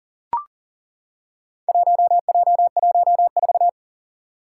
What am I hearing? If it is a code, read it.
Morse code, heard as a keyed tone, about 38 wpm, 700 Hz. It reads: 1J14
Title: Four Challenging Letter-Number Combinations 40wpm